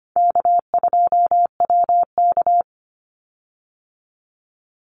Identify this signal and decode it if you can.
Morse code, heard as a keyed tone, about 25 wpm, 700 Hz. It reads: K2WK
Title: Call Signs Repeated 3x - 25wpm